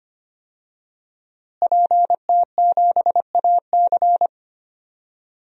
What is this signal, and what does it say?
Morse code, heard as a keyed tone, about 25 wpm, 700 Hz. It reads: PT7AC
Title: DXCC-100 Call Signs - 5 Characters - Repeated 3x - 25wpm